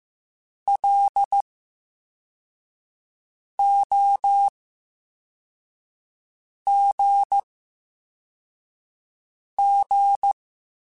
Morse code, heard as a keyed tone, about 15 wpm, 800 Hz. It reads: L O G G
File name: Special INGs-mp3s / Special INGs-mp3s / VerySlowToFaster Ing @13